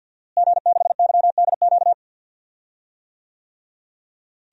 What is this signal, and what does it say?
Morse code, heard as a keyed tone, about 46 wpm, 700 Hz. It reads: G6YDQ
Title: DXCC-100 Call Signs - 5 Characters - Repeated 2x - 50wpm